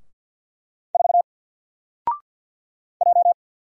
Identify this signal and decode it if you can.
Morse code, heard as a keyed tone, about 48 wpm, 700 Hz. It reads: 4 J
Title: Single Character (Letter:Number:Punctuation:Prosign) Alt 50wpm